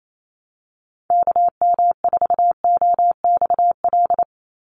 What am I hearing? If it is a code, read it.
Morse code, heard as a keyed tone, about 28 wpm, 700 Hz. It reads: KM4OXL
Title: USA Call Signs 28wpm